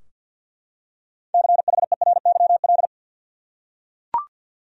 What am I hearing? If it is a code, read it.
Morse code, heard as a keyed tone, about 52 wpm, 700 Hz. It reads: CHERYL
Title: USA Common Names - 50wpm